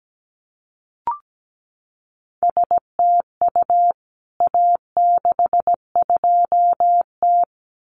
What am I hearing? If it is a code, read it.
Morse code, heard as a keyed tone, about 17 wpm, 700 Hz. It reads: STU A62T
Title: CWT Exchange 17wpm